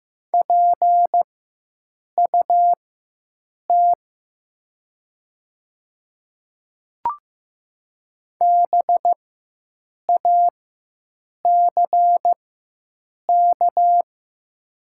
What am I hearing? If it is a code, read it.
Morse code, heard as a keyed tone, about 15 wpm, spaced as for 7 wpm, 700 Hz. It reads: PUT BACK
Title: 4x Character Spacing - Top 500 Words - 15wpm